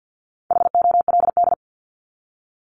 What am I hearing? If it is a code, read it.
Morse code, heard as a keyed tone, about 50 wpm, 700 Hz. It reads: HOLD
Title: ING Suffix - Top 100 Words with Base Words 50wpm